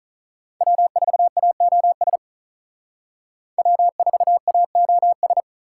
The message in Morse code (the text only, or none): W4AOS W4AOS